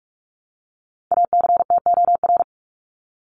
Morse code, heard as a keyed tone, about 44 wpm, 700 Hz. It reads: ACTOR
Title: Top 1251-1500 Words 45wpm